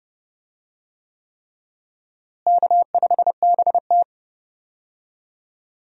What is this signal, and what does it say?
Morse code, heard as a keyed tone, about 30 wpm, 700 Hz. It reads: K5BT
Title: USA Call Signs - 4 Characters - Repeated 3x - 30wpm